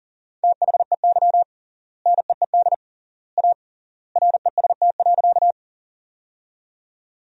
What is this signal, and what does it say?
Morse code, heard as a keyed tone, about 40 wpm, 700 Hz. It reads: THEY NEED A REST.
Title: 3x Word-Spacing - Sentences from Top 500 Words - 40wpm